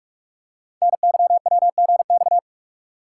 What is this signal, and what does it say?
Morse code, heard as a keyed tone, about 44 wpm, 700 Hz. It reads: NYWGX
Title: Code Groups - 5 Letter:Numbers - 45wpm